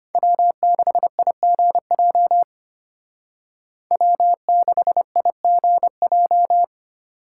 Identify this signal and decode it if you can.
Morse code, heard as a keyed tone, about 32 wpm, 700 Hz. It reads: W6IGJ W6IGJ